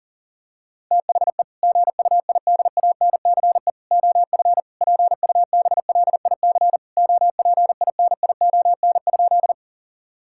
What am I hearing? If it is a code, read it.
Morse code, heard as a keyed tone, about 40 wpm, 700 Hz. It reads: THE GUIDANCE OF PUBLIC OPINION?
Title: Sentences from Top 1600 Words 40wpm